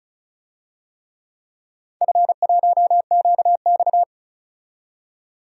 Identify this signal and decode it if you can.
Morse code, heard as a keyed tone, about 36 wpm, 700 Hz. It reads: F1QX
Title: DXCC-100 Call Signs - 4 Characters - 35wpm